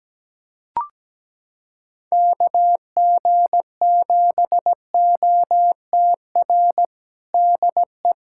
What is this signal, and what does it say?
Morse code, heard as a keyed tone, about 17 wpm, 700 Hz. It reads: KG7OTR DE
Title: USA Callsign DE Callsign - 6 Characters - 17wpm